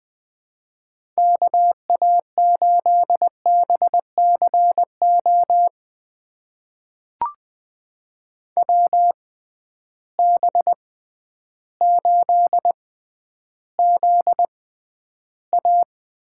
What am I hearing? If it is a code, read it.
Morse code, heard as a keyed tone, about 20 wpm, spaced as for 11 wpm, 700 Hz. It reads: KA8BCO WB8ZA
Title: USA Call Signs - 6 Characters - 6x Character Spacing - 20wpm